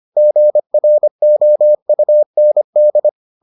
Morse code, heard as a keyed tone, about 25 wpm, 600 Hz. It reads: GROUND